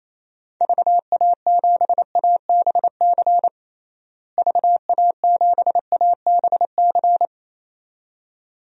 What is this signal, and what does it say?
Morse code, heard as a keyed tone, about 28 wpm, 700 Hz. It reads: VA7ABC VA7ABC
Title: DXCC-100 Call Signs - 6 Characters - Repeated 2x - 28wpm